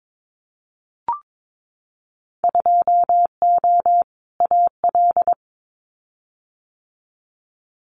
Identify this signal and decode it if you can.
Morse code, heard as a keyed tone, about 22 wpm, 700 Hz. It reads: 2O AL
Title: Winter Field Day Exchanges Encore 22wpm